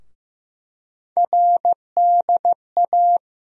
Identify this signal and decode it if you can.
Morse code, heard as a keyed tone, about 15 wpm, 700 Hz. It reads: RDA
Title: USA Call Sign Postfixes - 3 Letters 15wpm